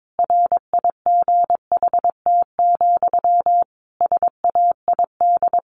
Morse code, heard as a keyed tone, about 22 wpm, 700 Hz. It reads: RIGHT, SAID
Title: A Sound Of Thunder 22wpm